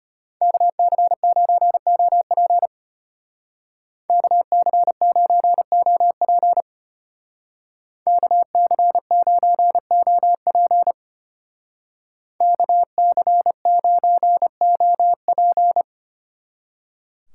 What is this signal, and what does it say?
Morse code, heard as a keyed tone, about 33 wpm, 700 Hz. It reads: KC9OP KC9OP KC9OP KC9OP